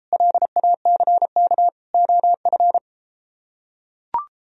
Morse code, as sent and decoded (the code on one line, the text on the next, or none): .-.. .- -.-. -.- / --- ..-.
LACK OF